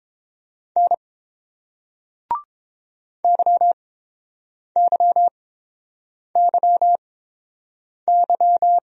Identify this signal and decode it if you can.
Morse code, heard as a keyed tone, about 31 wpm, spaced as for 7 wpm, 700 Hz. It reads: N YYYY